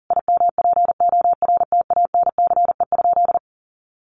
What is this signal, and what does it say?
Morse code, heard as a keyed tone, about 40 wpm, 700 Hz. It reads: IMPORTANCE?
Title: Sentences from Top 1200 Words 40wpm